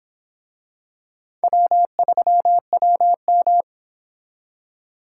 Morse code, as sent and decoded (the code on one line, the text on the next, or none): .-- ...-- .-- --
W3WM